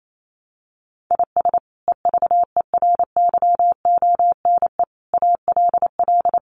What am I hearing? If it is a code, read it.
Morse code, heard as a keyed tone, about 28 wpm, 700 Hz. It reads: IS EVERYONE ALL
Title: Sentences from Top 800 Words 28wpm